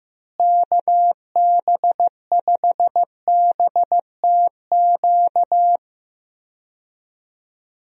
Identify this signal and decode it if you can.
Morse code, heard as a keyed tone, about 15 wpm, 700 Hz. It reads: KB5BTQ
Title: USA Call Signs - 6 Characters - Repeated 3x - 15wpm